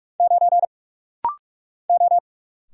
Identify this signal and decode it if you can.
Morse code, heard as a keyed tone, about 46 wpm, 700 Hz. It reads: 9 O
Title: Single Letter-Number Alt - Rapid-Fire 45wpm